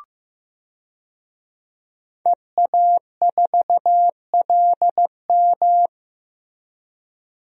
Morse code, heard as a keyed tone, about 15 wpm, 700 Hz. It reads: EA4LM
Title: DXCC-100 Call Signs - 5 Characters - Repeated 3x - 15wpm